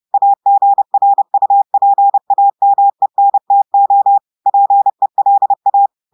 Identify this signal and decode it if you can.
Morse code, heard as a keyed tone, about 30 wpm, 800 Hz. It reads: AGRUPAMENTO PELA
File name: audios / portugais / portugais texte en clair 1 vitesse 30 mots minutes